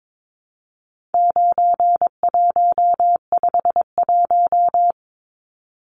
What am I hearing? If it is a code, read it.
Morse code, heard as a keyed tone, about 22 wpm, 700 Hz. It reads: 9151
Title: Numbers - 4-Digit 22wpm